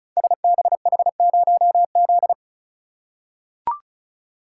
Morse code, heard as a keyed tone, about 36 wpm, 690 Hz. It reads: SBH0Z
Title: Code Groups - 5 Letter:Numbers - 35wpm